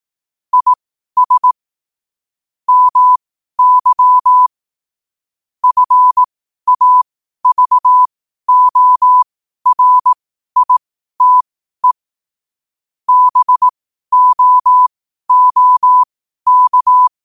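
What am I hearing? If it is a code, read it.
Morse code, heard as a keyed tone, about 18 wpm, spaced as for 12 wpm, 1000 Hz. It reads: IS MY FAVORITE BOOK
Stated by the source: Morse at 12 wpm.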